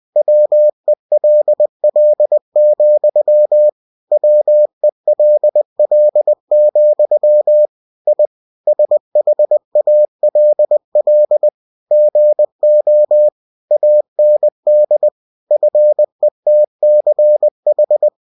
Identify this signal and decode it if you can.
Morse code, heard as a keyed tone, about 20 wpm, 600 Hz. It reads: WELL, WELL, I SHALL GO AND FETCH
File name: cw-books / winnie-the-pooh / winnie-20wpm-page-0017